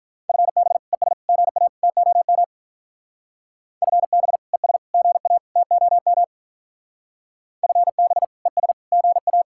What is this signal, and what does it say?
Morse code, heard as a keyed tone, about 42 wpm, 700 Hz. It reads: FB ES GA TOM FB ES GA TOM FB ES GA
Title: Speed-Racing Common QSO Phrases - 35wpm